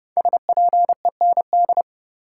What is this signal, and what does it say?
Morse code, heard as a keyed tone, about 30 wpm, 700 Hz. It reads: SPEND
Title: Sentences from Top 800 Words - New Word Drill - 30wpm